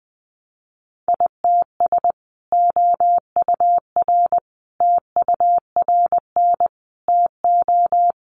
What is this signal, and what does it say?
Morse code, heard as a keyed tone, about 20 wpm, 700 Hz. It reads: ITS OUR TURN TO
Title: Sentences from Top 600 Words 20wpm